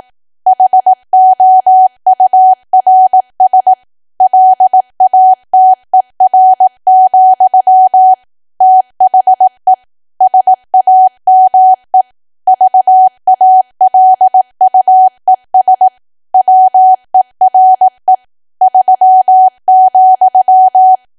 Morse code, heard as a keyed tone, about 18 wpm, 750 Hz. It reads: HOURS LATER, THE SAME VALUES WERE 3,
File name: cwa / W1AW-Level2 / 18WPM